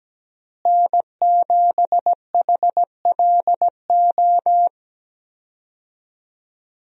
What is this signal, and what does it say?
Morse code, heard as a keyed tone, about 17 wpm, 700 Hz. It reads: N7HLO
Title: USA Call Signs - 5 Characters - Repeated 2x - 17wpm